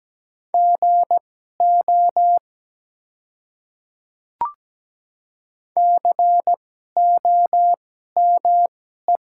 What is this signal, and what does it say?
Morse code, heard as a keyed tone, about 17 wpm, spaced as for 12 wpm, 700 Hz. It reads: GO COME
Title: 2x Character Spacing - Top 200 Words - 17wpm